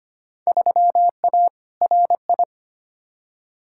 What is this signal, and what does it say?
Morse code, heard as a keyed tone, about 25 wpm, 700 Hz. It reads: 3A RI
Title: Field Day Exchanges - 25wpm